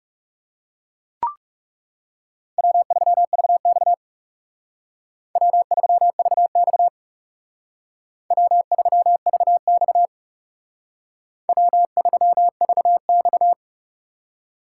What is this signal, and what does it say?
Morse code, heard as a keyed tone, about 39 wpm, 700 Hz. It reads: W3VX W3VX W3VX W3VX